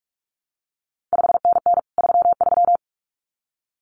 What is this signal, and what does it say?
Morse code, heard as a keyed tone, about 46 wpm, 700 Hz. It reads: 5NN 33
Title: CQ WW DX Contest Exchange - 45wpm